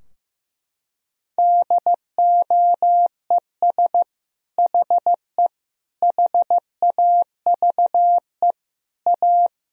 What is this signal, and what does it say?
Morse code, heard as a keyed tone, about 15 wpm, 700 Hz. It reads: DOES HE HAVE A
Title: Sentences from Top 1000 Words - Review Part 2 - 15wpm